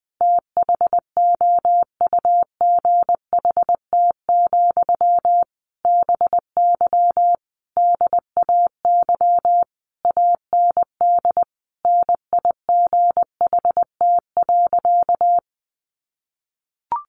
Morse code, heard as a keyed tone, about 20 wpm, 700 Hz. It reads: THOUGHT, BY DAY AND NIGHT.
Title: Sentences from Top 1500 Words 20wpm